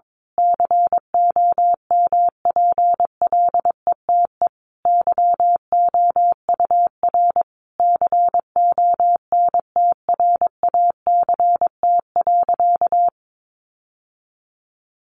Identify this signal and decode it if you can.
Morse code, heard as a keyed tone, about 22 wpm, 700 Hz. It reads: COMPLETE YOUR CONTRACT.
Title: Sentences from Top 1400 Words 22wpm